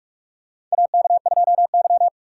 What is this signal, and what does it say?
Morse code, heard as a keyed tone, about 46 wpm, 700 Hz. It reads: AK2Y
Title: USA Call Signs - 4 Characters - 4x Character Spacing - 45wpm